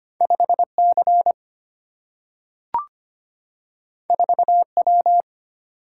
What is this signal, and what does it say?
Morse code, heard as a keyed tone, about 25 wpm, 700 Hz. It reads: 5 C 4 W
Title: Single-Letter Single-Number 25wpm